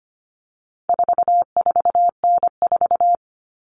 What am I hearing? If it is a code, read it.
Morse code, heard as a keyed tone, about 25 wpm, 700 Hz. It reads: 44N4
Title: All Cut Numbers - 4-Digit 25wpm